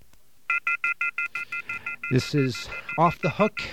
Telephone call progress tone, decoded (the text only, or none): off-hook warning tone